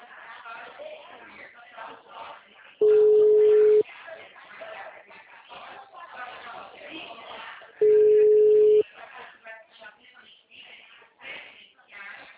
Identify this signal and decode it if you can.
Ringback tone (Europe, 425 Hz), with speech.
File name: out- -2014- - - .2703